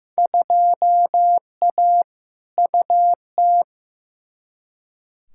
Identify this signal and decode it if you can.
Morse code, heard as a keyed tone, about 15 wpm, 700 Hz. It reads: 2A UT
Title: Field Day Exchanges - 15wpm